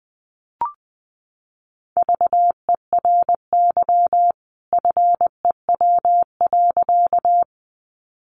Morse code, heard as a keyed tone, about 20 wpm, 700 Hz. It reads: VERY FEW.
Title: A Sound Of Thunder 20wpm